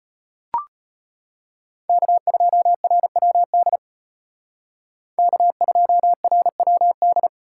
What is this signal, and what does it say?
Morse code, heard as a keyed tone, about 32 wpm, 700 Hz. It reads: K2RWD K2RWD